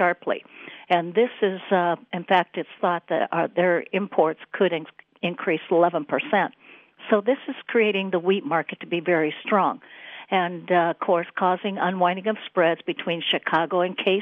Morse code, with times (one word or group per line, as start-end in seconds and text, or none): none